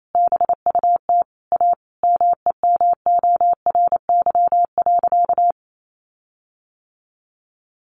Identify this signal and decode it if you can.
Morse code, heard as a keyed tone, about 28 wpm, 700 Hz. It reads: BUT A MEMORY.
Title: Sentences from Top 1300 Words 28wpm